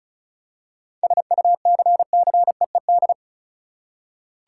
Morse code, heard as a keyed tone, about 36 wpm, 700 Hz. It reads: SUCCEED
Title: Seven Letter Words - 35wpm